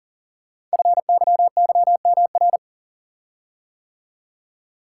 Morse code, heard as a keyed tone, about 42 wpm, 700 Hz. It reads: FYYMR
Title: Code Groups - 5 Letter:Numbers - 40wpm